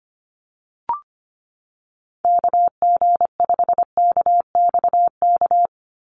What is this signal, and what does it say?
Morse code, heard as a keyed tone, about 25 wpm, 700 Hz. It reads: KG5KXK
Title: USA Call Signs 25wpm